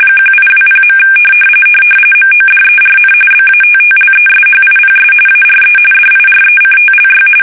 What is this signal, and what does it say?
RTTY, about 45.45 baud. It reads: 1820. MODERN BATTLESHIPS ARE DIRECTLY
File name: fsk850 / battleshipnj